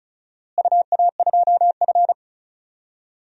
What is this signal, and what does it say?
Morse code, heard as a keyed tone, about 36 wpm, 700 Hz. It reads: UA2F
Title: DXCC-100 Call Signs - 4 Characters - Repeated 3x - 35wpm